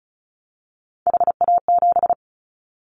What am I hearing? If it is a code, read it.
Morse code, heard as a keyed tone, about 36 wpm, 700 Hz. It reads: HA7
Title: Callsign Prefixes for Top 50 DXCC - 35wpm